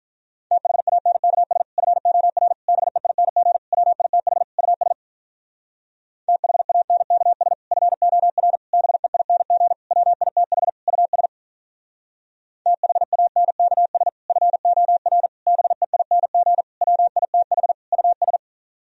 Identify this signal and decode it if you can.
Morse code, heard as a keyed tone, about 46 wpm, 700 Hz. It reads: THANKS FOR BEING WITH US THANKS FOR BEING WITH US THANKS FOR BEING WITH US